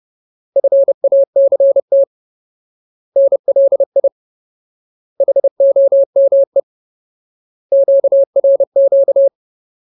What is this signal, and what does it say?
Morse code, heard as a keyed tone, about 30 wpm, 550 Hz. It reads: FACT NLI HOME QRQ
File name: Ultimate Word List / 30WPM Ultimate